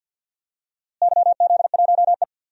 Morse code, heard as a keyed tone, about 52 wpm, 700 Hz. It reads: YZ1E
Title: DXCC-100 Call Signs - 4 Characters - Repeated 3x - 50wpm